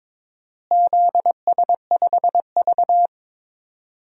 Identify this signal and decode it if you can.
Morse code, heard as a keyed tone, about 22 wpm, 700 Hz. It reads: ZS5V
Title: DXCC-100 Call Signs - 4 Characters - Repeated 2x - 22wpm